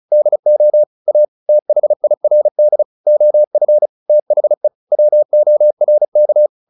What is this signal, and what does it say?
Morse code, heard as a keyed tone, about 34 wpm, 600 Hz. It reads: DO A THIRD OF THE WORK
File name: the old man and the sea / the old man and the sea 1 35wpm